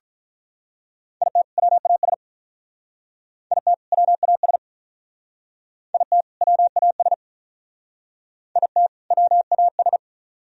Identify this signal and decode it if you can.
Morse code, heard as a keyed tone, about 43 wpm, 700 Hz. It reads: IT WAS IT WAS IT WAS IT WAS
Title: Speed-Racing - Sets of 2 Words 35wpm